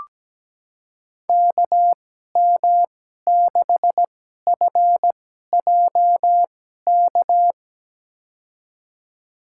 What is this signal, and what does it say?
Morse code, heard as a keyed tone, about 17 wpm, spaced as for 12 wpm, 700 Hz. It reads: KM6FJK
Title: USA Call Signs - 6 Characters - 2x Character Spacing - 17wpm